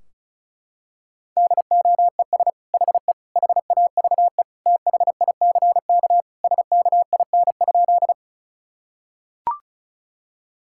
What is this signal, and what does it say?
Morse code, heard as a keyed tone, about 34 wpm, 700 Hz. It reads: DOES HE HAVE THICK SKIN?
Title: Sentences from Top 1600 Words 35wpm